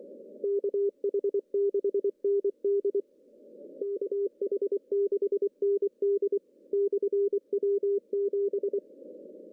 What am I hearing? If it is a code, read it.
Morse code, heard as a keyed tone, about 24 wpm, 400 Hz. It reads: KH6ND KH6ND/W7
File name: CW / cqww-cw-k5tr